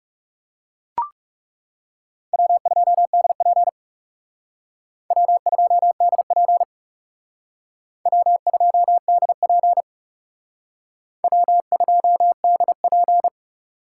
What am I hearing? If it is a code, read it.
Morse code, heard as a keyed tone, about 37 wpm, 700 Hz. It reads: W2DP W2DP W2DP W2DP
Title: Speed Racing - USA 1x2 Call Signs 30wpm